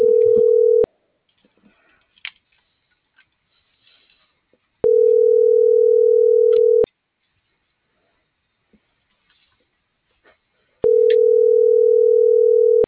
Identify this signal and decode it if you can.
Ringback tone (North America), 440 + 480 Hz.